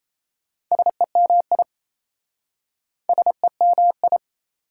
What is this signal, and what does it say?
Morse code, heard as a keyed tone, about 37 wpm, 700 Hz. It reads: SEMI SEMI